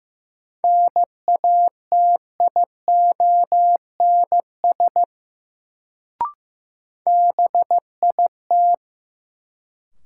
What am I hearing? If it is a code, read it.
Morse code, heard as a keyed tone, about 15 wpm, 700 Hz. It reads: NATIONS BIT
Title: Sets of 3 Words - New Word Drill - 15wpm